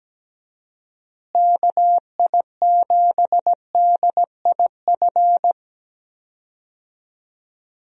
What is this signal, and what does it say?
Morse code, heard as a keyed tone, about 17 wpm, 700 Hz. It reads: KI7DIF
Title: USA Call Signs - 6 Characters - Repeated 3x - 17wpm